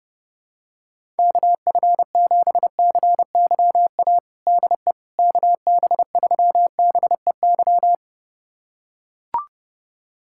Morse code, heard as a keyed tone, about 30 wpm, 700 Hz. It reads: KF7CYA DE KB3BEY
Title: USA Callsign DE Callsign - 6 Characters - 30wpm